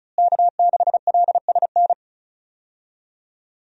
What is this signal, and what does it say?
Morse code, heard as a keyed tone, about 34 wpm, 700 Hz. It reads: K6LSN
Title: USA Call Signs - 5 Characters - Repeated 2x - 35wpm